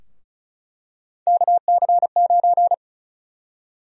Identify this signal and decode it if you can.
Morse code, heard as a keyed tone, about 34 wpm, 700 Hz. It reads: KC9
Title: USA Call Sign Prefixes - 35wpm